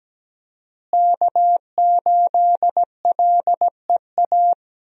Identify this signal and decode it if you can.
Morse code, heard as a keyed tone, about 17 wpm, 700 Hz. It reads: K8LEA
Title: USA Call Signs - 5 Characters - Repeated 3x - 17wpm